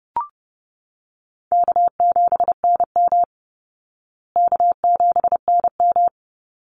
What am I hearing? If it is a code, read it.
Morse code, heard as a keyed tone, about 30 wpm, 700 Hz. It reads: K7NM K7NM
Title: Easy Call Signs Repeated 2x - 30wpm